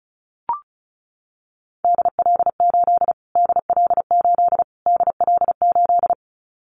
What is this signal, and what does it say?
Morse code, heard as a keyed tone, about 34 wpm, 700 Hz. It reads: DL8 DL8 DL8
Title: Callsign Prefixes for Top 25 DXCC - Repeated 3x - 35wpm